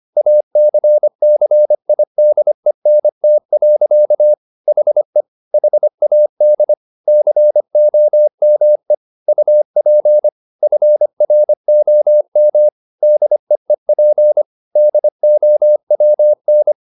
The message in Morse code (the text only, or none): ACCIDENT. HE HAD COME UP FROM DEEP DOWN